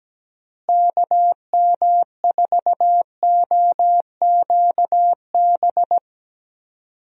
Morse code, heard as a keyed tone, about 17 wpm, 700 Hz. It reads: KM4OQB
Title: USA Call Signs - 6 Characters - Repeated 2x - 17wpm